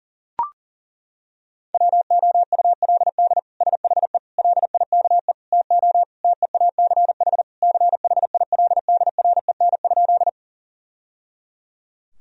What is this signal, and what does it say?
Morse code, heard as a keyed tone, about 40 wpm, 700 Hz. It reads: WOULD SHE LIKE TO TEACH CHILDREN?
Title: Sentences from Top 1000 Words - Review Part 2 - 40wpm